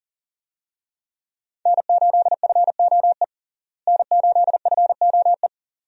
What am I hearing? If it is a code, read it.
Morse code, heard as a keyed tone, about 40 wpm, 700 Hz. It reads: N8FOE N8FOE
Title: USA Call Signs - 5 Characters - Repeated 2x - 40wpm